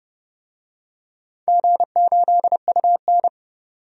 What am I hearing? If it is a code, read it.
Morse code, heard as a keyed tone, about 32 wpm, 700 Hz. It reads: G8UN
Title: DXCC-100 Call Signs - 4 Characters - 30wpm